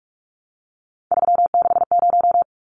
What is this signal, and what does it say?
Morse code, heard as a keyed tone, about 44 wpm, 700 Hz. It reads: 360
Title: Numbers - 3-Digit 45wpm